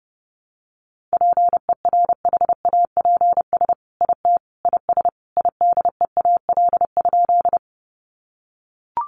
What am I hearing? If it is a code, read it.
Morse code, heard as a keyed tone, about 30 wpm, 700 Hz. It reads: PERHAPS IT IS IDEAL?
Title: Sentences from Top 1400 Words 30wpm